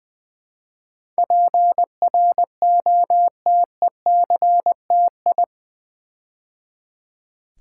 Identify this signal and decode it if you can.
Morse code, heard as a keyed tone, about 20 wpm, 700 Hz. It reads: PROTECTI
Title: Build A Word - 4 Letters At A Time - 10-Letter Words - 20wpm